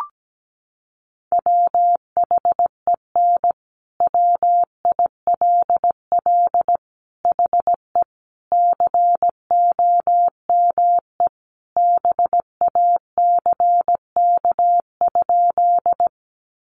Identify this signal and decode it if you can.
Morse code, heard as a keyed tone, about 17 wpm, 700 Hz. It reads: WHEN WILL HE COME BACK?
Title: Sentences from Top 200 Words 17wpm